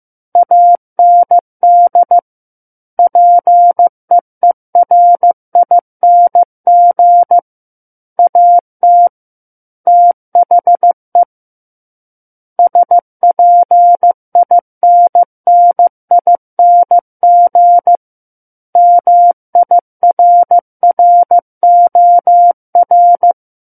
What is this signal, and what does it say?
Morse code, heard as a keyed tone, about 15 wpm, 700 Hz. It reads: AND PEERING AT THE SPINNING MIRROR
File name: Book One / 06-Chapter 6 - 15 WPM